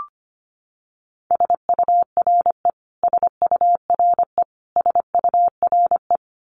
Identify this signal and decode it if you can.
Morse code, heard as a keyed tone, about 25 wpm, 700 Hz. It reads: SURE SURE SURE
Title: Top 500 Words Repeated 25wpm